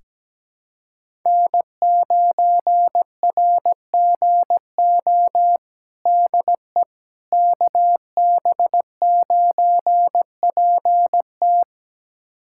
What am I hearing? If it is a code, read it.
Morse code, heard as a keyed tone, about 17 wpm, 700 Hz. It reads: N9RGO DE KB9PT
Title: USA Callsign DE Callsign - 5 Characters - 17wpm